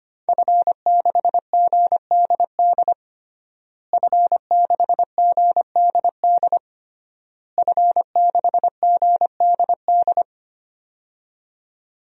Morse code, heard as a keyed tone, about 25 wpm, 700 Hz. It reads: F6GDD F6GDD F6GDD
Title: DXCC-100 Call Signs - 5 Characters - Repeated 3x - 25wpm